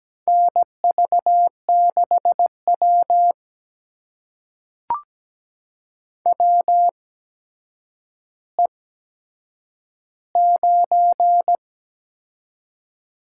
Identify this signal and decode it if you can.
Morse code, heard as a keyed tone, about 17 wpm, spaced as for 8 wpm, 700 Hz. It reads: NV6W WE9
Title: USA Call Signs - 4 Characters - 8x Character Spacing - 17wpm